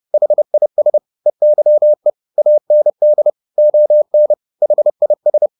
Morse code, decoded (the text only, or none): HIS EYE AND ON HIS